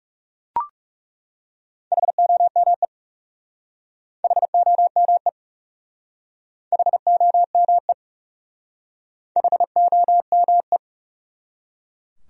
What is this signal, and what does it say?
Morse code, heard as a keyed tone, about 36 wpm, 700 Hz. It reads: HOME HOME HOME HOME
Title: Speed-Racing Top 200 Words 30wpm